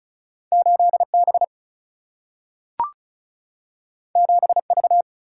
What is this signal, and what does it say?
Morse code, heard as a keyed tone, about 36 wpm, 700 Hz. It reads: 8B 7V
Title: Single-Letter Single-Number 35wpm